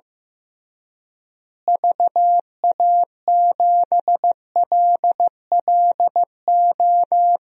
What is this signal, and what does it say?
Morse code, heard as a keyed tone, about 15 wpm, 700 Hz. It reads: VA7LLO
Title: DXCC-100 Call Signs - 6 Characters - Repeated 2x - 15wpm